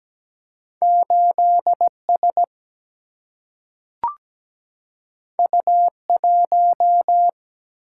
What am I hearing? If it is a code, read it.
Morse code, heard as a keyed tone, about 17 wpm, 700 Hz. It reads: 8 S U 1
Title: Single-Letter Single-Number 17wpm